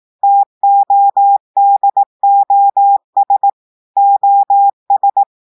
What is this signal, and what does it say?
Morse code, heard as a keyed tone, about 18 wpm, 800 Hz. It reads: TODOS OS
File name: audios / portugais / portugais texte en clair 2 vitesse 18 mots minutes